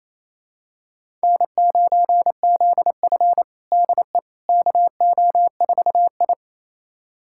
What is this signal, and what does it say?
Morse code, heard as a keyed tone, about 28 wpm, 700 Hz. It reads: N9ZF DE KO4I
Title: USA Callsign DE Callsign - 4 Characters - 28wpm